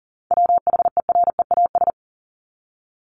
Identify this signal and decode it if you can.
Morse code, heard as a keyed tone, about 40 wpm, 700 Hz. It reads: WHEREAS
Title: Top 1501-1750 Words 40wpm